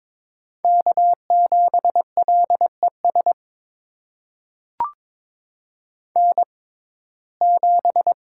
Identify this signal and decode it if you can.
Morse code, heard as a keyed tone, about 22 wpm, spaced as for 16 wpm, 700 Hz. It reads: K7LES N7
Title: USA Call Signs - 5 Characters - 6x Character Spacing - 22wpm